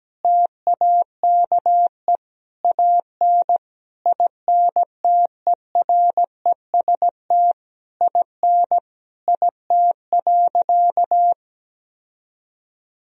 Morse code, heard as a keyed tone, about 17 wpm, 700 Hz. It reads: TAKE AN INTEREST IN IT.
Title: Sentences from Top 1000 Words - Review Part 2 - 17wpm